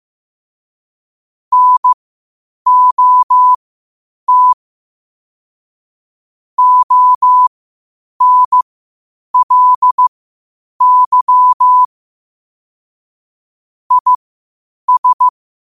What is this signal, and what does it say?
Morse code, heard as a keyed tone, about 15 wpm, spaced as for 8 wpm, 1000 Hz. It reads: NOT ONLY IS